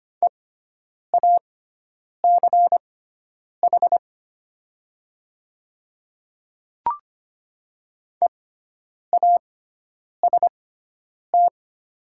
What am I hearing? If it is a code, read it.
Morse code, heard as a keyed tone, about 25 wpm, spaced as for 9 wpm, 700 Hz. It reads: EACH EAST
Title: 6x Character Spacing - Top 500 Words - 25wpm